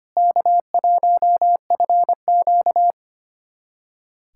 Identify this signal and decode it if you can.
Morse code, heard as a keyed tone, about 25 wpm, 700 Hz. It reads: K1FQ
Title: USA Callsign DE Callsign - 4 Characters - 25wpm